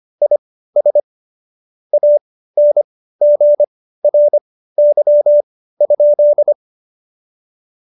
Morse code, heard as a keyed tone, about 25 wpm, spaced as for 15 wpm, 590 Hz. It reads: IS ANGRY?
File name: Jokes-25-char-15-eff